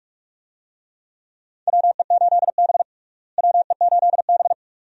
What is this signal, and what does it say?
Morse code, heard as a keyed tone, about 43 wpm, 700 Hz. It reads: WE8B WE8B
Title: USA Call Signs - 4 Characters - Repeated 2x - 45wpm